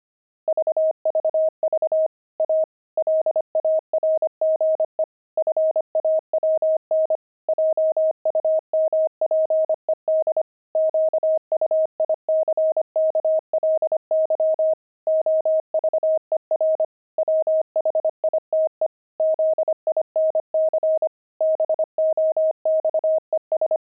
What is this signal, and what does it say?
Morse code, heard as a keyed tone, about 25 wpm, 630 Hz. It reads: VVV A LARGE FAWN JUMPED QUICKLY OVER WHITE ZINC BOXES
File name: pangram-fs-25-cs-25-fawn